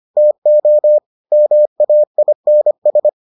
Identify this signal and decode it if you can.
Morse code, heard as a keyed tone, about 25 wpm, 600 Hz. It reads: TO MAINS